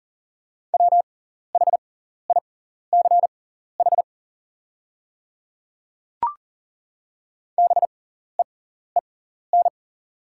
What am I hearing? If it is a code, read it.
Morse code, heard as a keyed tone, about 40 wpm, spaced as for 14 wpm, 700 Hz. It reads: WHICH BEEN
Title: 6x Character Spacing - Top 100 Words - 40wpm